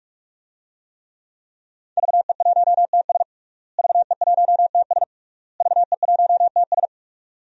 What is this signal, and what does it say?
Morse code, heard as a keyed tone, about 46 wpm, 700 Hz. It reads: VE1TS VE1TS VE1TS
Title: DXCC-100 Call Signs - 5 Characters - Repeated 3x - 45wpm